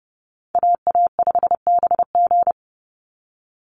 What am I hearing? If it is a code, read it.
Morse code, heard as a keyed tone, about 30 wpm, 700 Hz. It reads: AA5BG
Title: Call Signs Repeated 3x - 30wpm